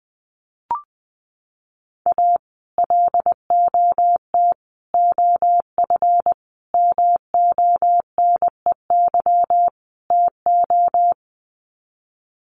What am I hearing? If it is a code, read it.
Morse code, heard as a keyed tone, about 20 wpm, 700 Hz. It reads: A LOT OF MONEY TO
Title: Sets of 5 Words Encore2 20wpm